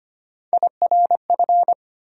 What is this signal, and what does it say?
Morse code, heard as a keyed tone, about 25 wpm, 700 Hz. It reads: IRF
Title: USA Call Sign Postfixes - 3 Letters 25wpm